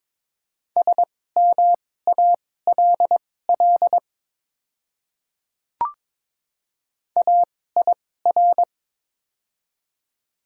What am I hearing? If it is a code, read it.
Morse code, heard as a keyed tone, about 22 wpm, spaced as for 16 wpm, 700 Hz. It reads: SMALL AIR
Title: 2x Character Spacing - Top 200 Words - 22wpm